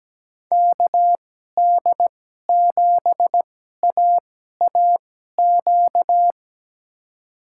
Morse code, heard as a keyed tone, about 17 wpm, spaced as for 12 wpm, 700 Hz. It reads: KD7AAQ
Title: USA Call Signs - 2x Character Spacing - 17wpm